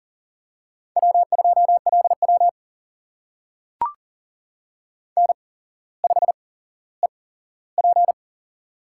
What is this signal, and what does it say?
Morse code, heard as a keyed tone, about 42 wpm, spaced as for 17 wpm, 700 Hz. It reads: W2LW N5EP
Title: USA Call Signs - 4 Characters - 8x Character Spacing - 40wpm